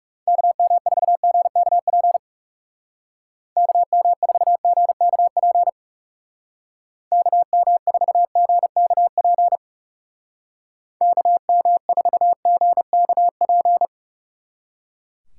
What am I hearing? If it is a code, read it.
Morse code, heard as a keyed tone, about 35 wpm, 700 Hz. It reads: KM4GKP KM4GKP KM4GKP KM4GKP
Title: Speed Racing - USA Call Signs 30wpm